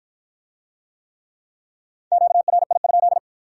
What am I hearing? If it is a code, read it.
Morse code, heard as a keyed tone, about 53 wpm, 700 Hz. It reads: QRI?